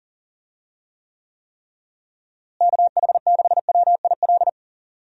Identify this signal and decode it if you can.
Morse code, heard as a keyed tone, about 40 wpm, 700 Hz. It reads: KH6WIL
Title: USA Call Signs - 6 Characters - Repeated 3x - 40wpm